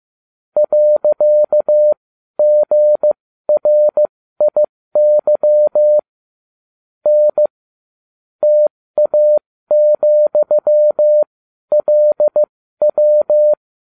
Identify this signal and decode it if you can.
Morse code, heard as a keyed tone, about 15 wpm, spaced as for 12 wpm, 600 Hz. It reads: .GRIY N TA,LW